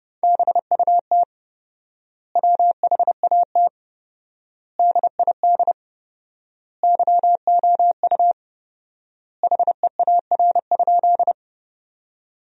Morse code, heard as a keyed tone, about 30 wpm, 700 Hz. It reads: BUT WHAT DID YOU HEAR?
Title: 4x Word-Spacing - Sentences from Top 300 Words - 30wpm